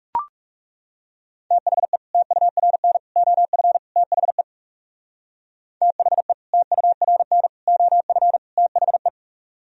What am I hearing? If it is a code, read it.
Morse code, heard as a keyed tone, about 37 wpm, 690 Hz. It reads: THE TURN OF THE THE TURN OF THE
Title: Speed-Racing - Sets of 4 Words 30wpm